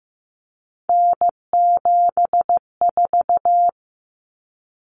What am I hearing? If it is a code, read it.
Morse code, heard as a keyed tone, about 15 wpm, 700 Hz. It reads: N74
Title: Common Cut Numbers - 3-Digit 15wpm